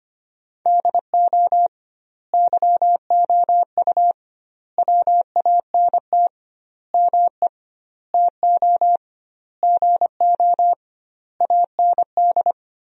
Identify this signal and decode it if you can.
Morse code, heard as a keyed tone, about 25 wpm, 700 Hz. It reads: DO YOU WANT ME TO GO AND
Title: 2x Word-Spacing - Sentences from Top 200 Words - 25wpm